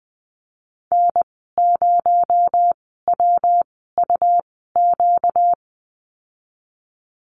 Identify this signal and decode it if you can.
Morse code, heard as a keyed tone, about 20 wpm, spaced as for 14 wpm, 700 Hz. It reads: N0WUQ
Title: 245 Lesson 39g - Letter z - Call Signs 20wpm